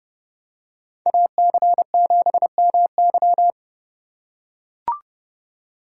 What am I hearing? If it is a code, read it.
Morse code, heard as a keyed tone, about 30 wpm, 700 Hz. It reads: AC7MY
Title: USA Call Signs - 2x Character Spacing - 30wpm